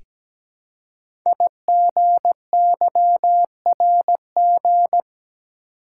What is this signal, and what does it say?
Morse code, heard as a keyed tone, about 17 wpm, 700 Hz. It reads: IGYRG
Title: Code Groups - 5 Letters - 17wpm